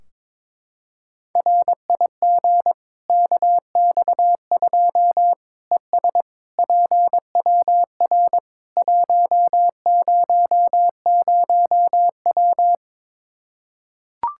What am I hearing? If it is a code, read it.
Morse code, heard as a keyed tone, about 22 wpm, 700 Hz. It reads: RIG KX2 ES PWR 100W
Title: Practice QSO - Exchange 2 - Info - 22wpm